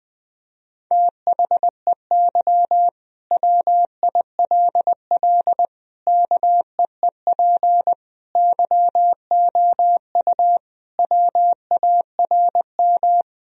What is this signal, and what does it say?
Morse code, heard as a keyed tone, about 21 wpm, 700 Hz. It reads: THEY WILL KEEP YOU WARM
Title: Sentences from Top 1000 Words - Review Part 3 - 20wpm